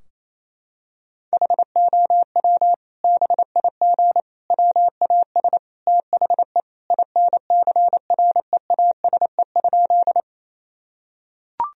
Morse code, heard as a keyed tone, about 28 wpm, 700 Hz. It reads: HOW BIG WAS THE INCREASE?
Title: Sentences from Top 1000 Words - Review Part 4 - 28wpm